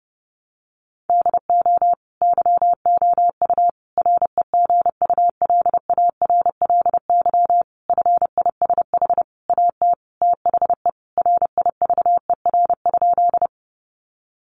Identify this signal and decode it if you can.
Morse code, heard as a keyed tone, about 30 wpm, 700 Hz. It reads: DO YOU REGULARLY FISH AT THE RIVER?
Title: Sentences from Top 1400 Words 30wpm